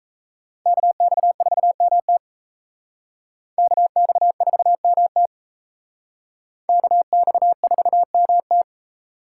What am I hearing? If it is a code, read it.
Morse code, heard as a keyed tone, about 35 wpm, 700 Hz. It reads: KX4MT KX4MT KX4MT